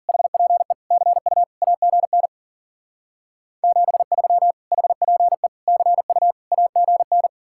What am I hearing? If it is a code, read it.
Morse code, heard as a keyed tone, about 51 wpm, 700 Hz. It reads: HPE CU AGN 73 HPE CU AGN